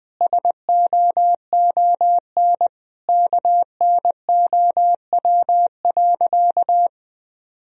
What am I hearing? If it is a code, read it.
Morse code, heard as a keyed tone, about 21 wpm, 700 Hz. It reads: SOON KNOW.
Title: Sentences from Top 400 Words and Abbreviations 20wpm